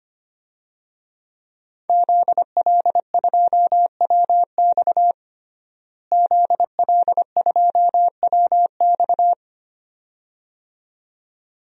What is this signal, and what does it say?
Morse code, heard as a keyed tone, about 25 wpm, 700 Hz. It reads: ZL2WX ZL2WX
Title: DXCC-100 Call Signs - 5 Characters - Repeated 2x - 25wpm